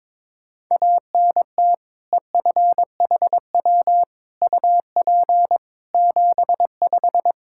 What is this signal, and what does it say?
Morse code, heard as a keyed tone, about 22 wpm, 700 Hz. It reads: ANT EFHW UP 75
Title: Practice QSO - Exchange 2 - Info - 22wpm